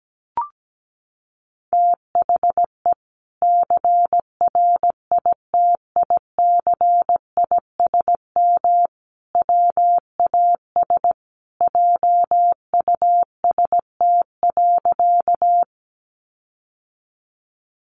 Morse code, heard as a keyed tone, about 17 wpm, 700 Hz. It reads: THE CRITICISM WAS JUST.
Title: Sentences from Top 1300 Words 17wpm